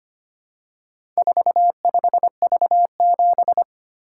Morse code, heard as a keyed tone, about 25 wpm, 700 Hz. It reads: 45V7
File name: practice / 25wpm / Four Letter-Number Combinations - 25wpm